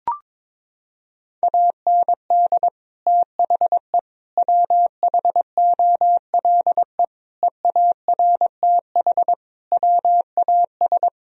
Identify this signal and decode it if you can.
Morse code, heard as a keyed tone, about 22 wpm, 700 Hz. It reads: AND THE WHOLE EARTH WAS
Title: Sentences from Top 1000 Words - Review Part 3 - 22wpm